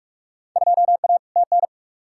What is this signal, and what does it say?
Morse code, heard as a keyed tone, about 44 wpm, 700 Hz. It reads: 2A TN
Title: Field Day Exchanges - 45wpm